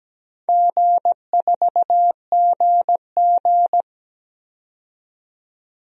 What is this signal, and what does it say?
Morse code, heard as a keyed tone, about 17 wpm, 700 Hz. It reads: G4GG
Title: DXCC-100 Call Signs - 4 Characters - 17wpm